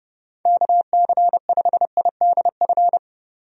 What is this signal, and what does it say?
Morse code, heard as a keyed tone, about 32 wpm, 700 Hz. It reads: KC5IDF
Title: USA Call Signs - 6 Characters - 30wpm